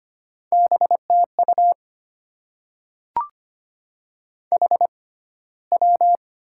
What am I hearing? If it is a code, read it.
Morse code, heard as a keyed tone, about 25 wpm, spaced as for 15 wpm, 700 Hz. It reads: BTU HW
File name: Top 20 QSO Elements - 6x Character Spacing - 25wpm